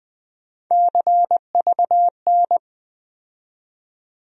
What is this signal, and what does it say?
Morse code, heard as a keyed tone, about 20 wpm, 700 Hz. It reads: CVN
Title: USA Call Sign Postfixes - 3 Letters 20wpm